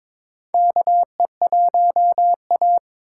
Morse code, heard as a keyed tone, about 21 wpm, 700 Hz. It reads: KE1A
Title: USA Call Signs - 4 Characters - 6x Character Spacing - 22wpm